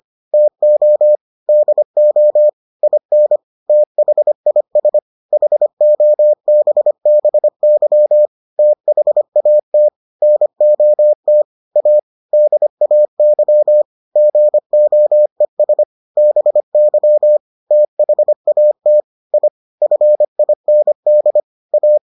Morse code, heard as a keyed tone, about 25 wpm, 600 Hz. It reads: TO DO IN THIS HOBBY THAT NOT A DAY GOES BY THAT I FIND A